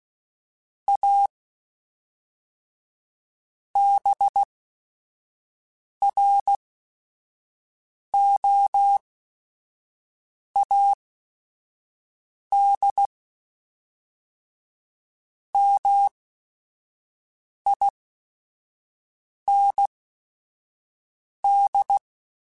Morse code, heard as a keyed tone, about 16 wpm, spaced as for 5 wpm, 800 Hz. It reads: A BROAD MIND
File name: Chuckles / Chuckles-mp3s / MiddleAge @5